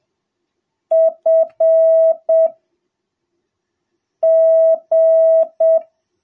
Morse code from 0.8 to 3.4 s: F